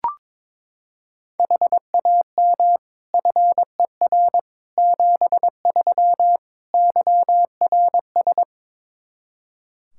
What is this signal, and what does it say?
Morse code, heard as a keyed tone, about 22 wpm, 700 Hz. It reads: HAM FER 73 YRS
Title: Practice QSO - Exchange 3 - More Info - 22wpm